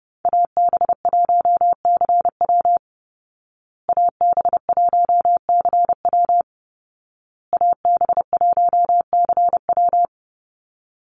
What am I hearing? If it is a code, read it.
Morse code, heard as a keyed tone, about 30 wpm, 700 Hz. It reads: AB1CW AB1CW AB1CW
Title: Call Signs Repeated 2x - 30wpm